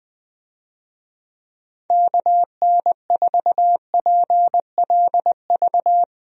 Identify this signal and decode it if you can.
Morse code, heard as a keyed tone, about 21 wpm, 700 Hz. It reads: KN4PLV
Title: USA Call Signs - 6 Characters - Repeated 3x - 20wpm